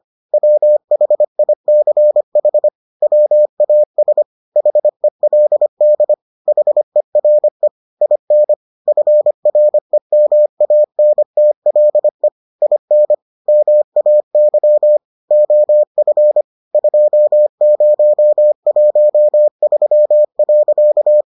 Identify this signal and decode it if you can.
Morse code, heard as a keyed tone, about 25 wpm, 600 Hz. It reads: WHICH WAS HELD HERE IN FREMANTLE IN MAY OF 2013.